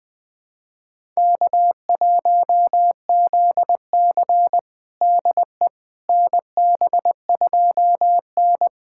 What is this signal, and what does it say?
Morse code, heard as a keyed tone, about 21 wpm, 700 Hz. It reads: K1ZC DE NB2N
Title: USA Callsign DE Callsign - 4 Characters - 20wpm